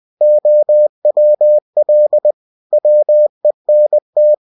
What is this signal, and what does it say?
Morse code, heard as a keyed tone, about 20 wpm, 600 Hz. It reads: OWL WENT
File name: winnie-20wpm